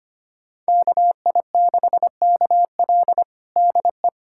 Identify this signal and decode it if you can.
Morse code, heard as a keyed tone, about 25 wpm, 700 Hz. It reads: KI6KL DE
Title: USA Callsign DE Callsign 25wpm